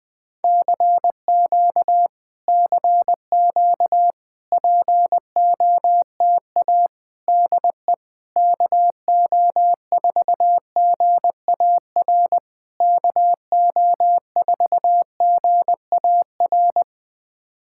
Morse code, heard as a keyed tone, about 20 wpm, 700 Hz. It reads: CQ CQ POTA DE KO4GAR KO4GAR
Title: POTA QSO Practice - 20wpm